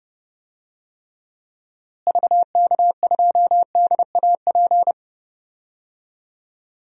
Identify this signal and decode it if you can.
Morse code, heard as a keyed tone, about 32 wpm, 700 Hz. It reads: VK2DAP
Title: DXCC-100 Call Signs - 6 Characters - Repeated 3x - 30wpm